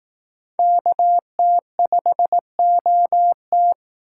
Morse code, heard as a keyed tone, about 18 wpm, 700 Hz. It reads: KT5OT
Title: Speed Racing - USA 2x2 Call Signs 15wpm